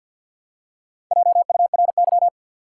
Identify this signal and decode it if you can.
Morse code, heard as a keyed tone, about 47 wpm, 700 Hz. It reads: JURY
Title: Four Letter Words - 50wpm